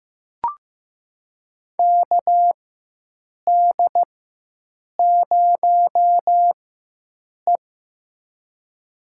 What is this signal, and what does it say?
Morse code, heard as a keyed tone, about 15 wpm, spaced as for 7 wpm, 700 Hz. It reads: KD0E
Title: USA Call Signs - 4x Character Spacing - 15wpm